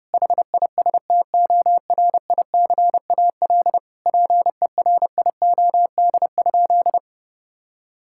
Morse code, heard as a keyed tone, about 30 wpm, 700 Hz. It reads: HISTORICAL PERIOD?
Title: Sentences from Top 1000 Words - Review Part 5 - 30wpm